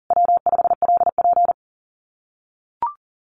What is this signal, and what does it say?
Morse code, heard as a keyed tone, about 40 wpm, 700 Hz. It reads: W5LP
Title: Common QSO Phrases Encore 40wpm